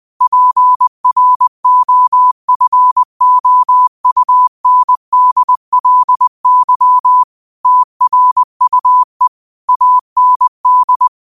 Morse code, heard as a keyed tone, about 20 wpm, 1000 Hz. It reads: PROFOUNDLY TRUE AND
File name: qotdfiles / 20wpm